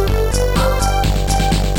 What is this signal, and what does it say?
There is no code.